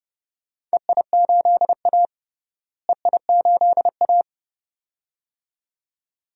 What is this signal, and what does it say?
Morse code, heard as a keyed tone, about 30 wpm, 700 Hz. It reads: EI8A EI8A
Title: DXCC-100 Call Signs - 4 Characters - Repeated 2x - 30wpm